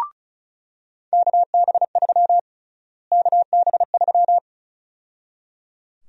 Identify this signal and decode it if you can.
Morse code, heard as a keyed tone, about 34 wpm, 700 Hz. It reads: KB3 KB3
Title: USA Call Sign Prefixes - Repeated 2x - 35wpm